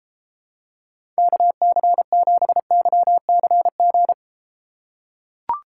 Morse code, heard as a keyed tone, about 32 wpm, 700 Hz. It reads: KC7YCG